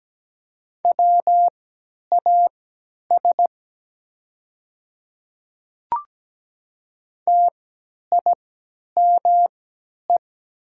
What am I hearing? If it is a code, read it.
Morse code, heard as a keyed tone, about 17 wpm, spaced as for 10 wpm, 690 Hz. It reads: WAS TIME
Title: 3x Character Spacing - Top 100 Words - 17wpm